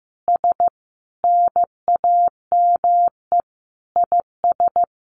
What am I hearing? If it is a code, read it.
Morse code, heard as a keyed tone, about 15 wpm, 700 Hz. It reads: S NAME IS
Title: Easy Phrases 15wpm